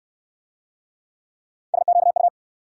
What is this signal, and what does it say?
Morse code, heard as a keyed tone, about 73 wpm, 690 Hz. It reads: SQU